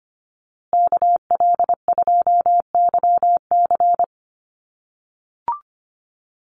Morse code, heard as a keyed tone, about 25 wpm, 700 Hz. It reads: KL2YC
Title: USA Call Signs Repeated 3x 25wpm